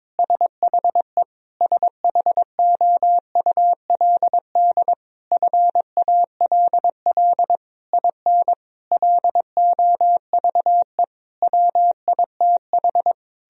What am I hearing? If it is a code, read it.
Morse code, heard as a keyed tone, about 22 wpm, 700 Hz. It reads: SHE SHOULD FALL IN LOVE WITH
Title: Sentences from Top 1000 Words - Review Part 4 - 22wpm